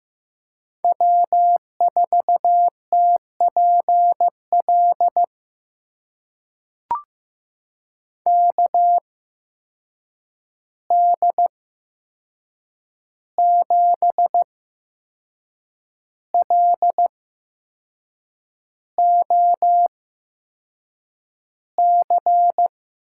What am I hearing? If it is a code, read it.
Morse code, heard as a keyed tone, about 15 wpm, spaced as for 6 wpm, 700 Hz. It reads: W4TPL KD7LOC
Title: USA Call Signs - 8x Character Spacing - 15wpm